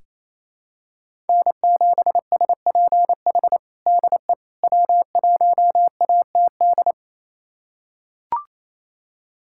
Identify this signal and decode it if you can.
Morse code, heard as a keyed tone, about 27 wpm, 700 Hz. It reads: N7SPH DE W1ATD
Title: USA Callsign DE Callsign - 5 Characters - 28wpm